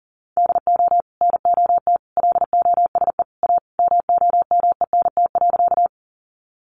Morse code, heard as a keyed tone, about 40 wpm, 700 Hz. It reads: DO NOT LOSE A MOMENT.
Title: Sentences from Top 1100 Words 40wpm